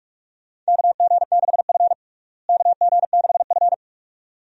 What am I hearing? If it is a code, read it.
Morse code, heard as a keyed tone, about 43 wpm, 700 Hz. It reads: KG6F KG6F
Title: USA Call Signs - 4 Characters - Repeated 2x - 45wpm